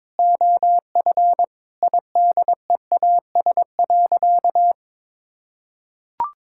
Morse code, heard as a keyed tone, about 22 wpm, 700 Hz. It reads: OF IDEAS.
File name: practice / 22wpm / Sentences with 4-Word Phrases - 22wpm